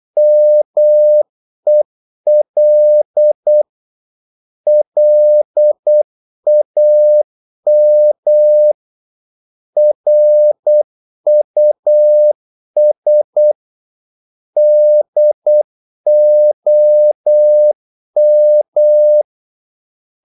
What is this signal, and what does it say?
Morse code, heard as a keyed tone, about 8 wpm, 600 Hz. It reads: MEL LAM RUS DOM
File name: Korte ord DK 8wpm